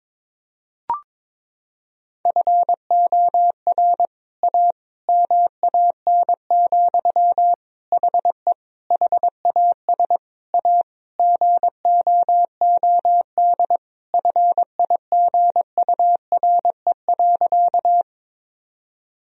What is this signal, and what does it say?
Morse code, heard as a keyed tone, about 21 wpm, 700 Hz. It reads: FOR A MAN, HE HAS A GOOD FIGURE.
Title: Sentences from Top 1000 Words - Review Part 2 - 22wpm